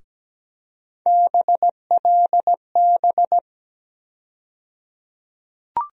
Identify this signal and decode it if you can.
Morse code, heard as a keyed tone, about 17 wpm, 700 Hz. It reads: BLB
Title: USA Call Sign Postfixes - 3 Letters 17wpm